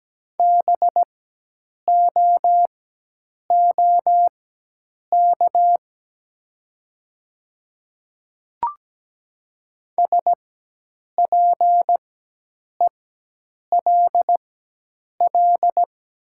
Morse code, heard as a keyed tone, about 17 wpm, spaced as for 8 wpm, 700 Hz. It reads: BOOK SPELL
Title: 4x Character Spacing - Top 300 Words - 17wpm